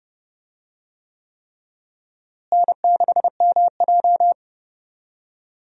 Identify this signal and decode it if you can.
Morse code, heard as a keyed tone, about 30 wpm, 700 Hz. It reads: N6MJ
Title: USA Call Signs - 4 Characters - Repeated 3x - 30wpm